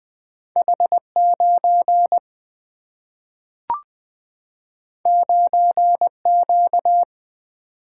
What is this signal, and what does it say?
Morse code, heard as a keyed tone, about 20 wpm, 700 Hz. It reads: H9 9Q
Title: Single-Letter Single-Number 20wpm